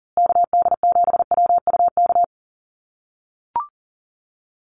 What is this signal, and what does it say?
Morse code, heard as a keyed tone, about 40 wpm, 700 Hz. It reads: KD7WUK
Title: USA Call Signs Repeated 3x 40wpm